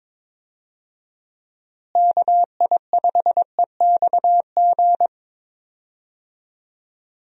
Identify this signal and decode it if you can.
Morse code, heard as a keyed tone, about 22 wpm, 700 Hz. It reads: KI5EXG
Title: USA Call Signs - 6 Characters - Repeated 2x - 22wpm